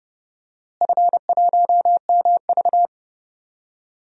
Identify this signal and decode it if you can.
Morse code, heard as a keyed tone, about 30 wpm, 700 Hz. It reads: F1MV
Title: DXCC-100 Call Signs - 4 Characters - Repeated 2x - 30wpm